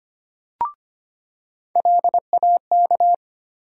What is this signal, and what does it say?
Morse code, heard as a keyed tone, about 25 wpm, 700 Hz. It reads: LAK
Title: USA Call Sign Postfixes - 3 Letters 25wpm